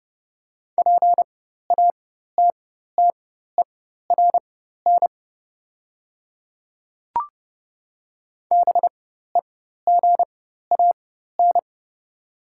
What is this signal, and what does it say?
Morse code, heard as a keyed tone, about 30 wpm, spaced as for 14 wpm, 690 Hz. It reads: PATTERN BEGAN
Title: 4x Character Spacing - Top 500 Words - 30wpm